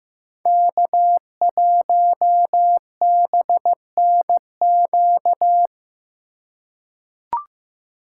Morse code, heard as a keyed tone, about 15 wpm, 700 Hz. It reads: K1BNQ
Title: USA Call Signs - 5 Characters - 4x Character Spacing - 15wpm